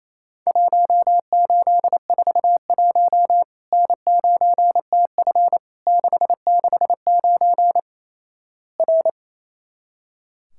Morse code, heard as a keyed tone, about 27 wpm, 700 Hz. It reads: 1841 N9TF 669 R
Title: Worked All Europe DX Contest - QTC Exchange - 28wpm